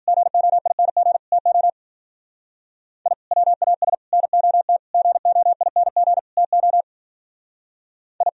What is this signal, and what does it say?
Morse code, heard as a keyed tone, about 52 wpm, 700 Hz. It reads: GOING TO I WAS NOT GOING TO I